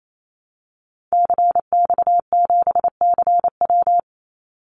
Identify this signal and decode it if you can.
Morse code, heard as a keyed tone, about 28 wpm, 700 Hz. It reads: CX7CW
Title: Call Signs Repeated 3x - 28wpm